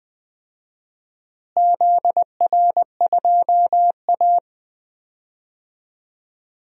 Morse code, heard as a keyed tone, about 20 wpm, 690 Hz. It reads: ZR2A
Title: DXCC-100 Call Signs - 4 Characters - Repeated 3x - 20wpm